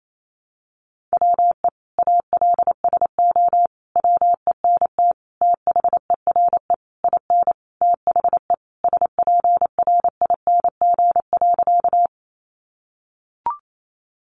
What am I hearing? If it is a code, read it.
Morse code, heard as a keyed tone, about 28 wpm, 700 Hz. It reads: WE ALSO WENT THERE IN THE SPRING.
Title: Sentences from Top 1200 Words 28wpm